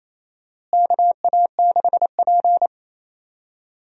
Morse code, heard as a keyed tone, about 28 wpm, 700 Hz. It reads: KA6P
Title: USA Call Signs - 4 Characters - Repeated 2x - 28wpm